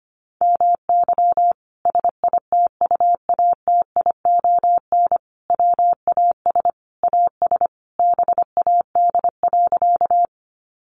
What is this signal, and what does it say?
Morse code, heard as a keyed tone, about 25 wpm, 700 Hz. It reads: MY SITUATION WAS AS BAD.
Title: Sentences from Top 1200 Words 25wpm